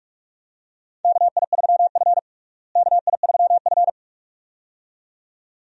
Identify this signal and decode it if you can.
Morse code, heard as a keyed tone, about 44 wpm, 700 Hz. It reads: KI3F KI3F
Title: USA Call Signs - 4 Characters - Repeated 2x - 45wpm